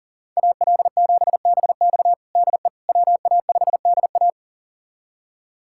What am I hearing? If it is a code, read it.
Morse code, heard as a keyed tone, about 37 wpm, 700 Hz. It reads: AL7BX DE WA5DA
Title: USA Callsign DE Callsign - 5 Characters - 40wpm